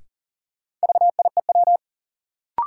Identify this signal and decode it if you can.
Morse code, heard as a keyed tone, about 40 wpm, 700 Hz. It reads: VIEW
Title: Sentences from Top 900 Words - New Word Drill - 40wpm